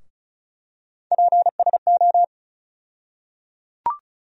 Morse code, heard as a keyed tone, about 34 wpm, 700 Hz. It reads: PSO
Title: USA Call Sign Postfixes - 3 Letters 35wpm